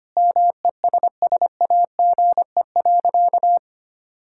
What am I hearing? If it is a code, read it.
Morse code, heard as a keyed tone, about 25 wpm, 700 Hz. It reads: MESSAGE.
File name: practice / 25wpm / Sentences with 9-Word Phrases - 25wpm